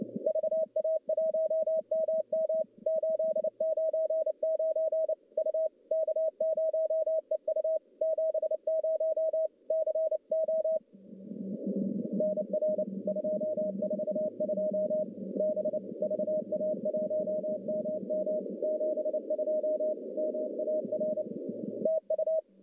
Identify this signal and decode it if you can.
Morse code, heard as a keyed tone, about 29 wpm, 620 Hz. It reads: VA1MM 899 U K0EU 70 CO NR 242 B VA1MM 72 MAR TU
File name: contests / CW-SS-k5tr- -1100